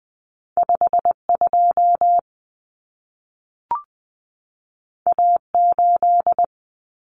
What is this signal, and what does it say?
Morse code, heard as a keyed tone, about 20 wpm, 700 Hz. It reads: 52 A8
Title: Common Cut Numbers - 2-Digit 20wpm